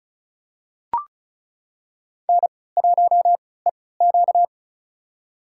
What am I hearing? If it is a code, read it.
Morse code, heard as a keyed tone, about 34 wpm, spaced as for 20 wpm, 700 Hz. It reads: N1EQ
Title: USA Call Signs - 3x Character Spacing - 35wpm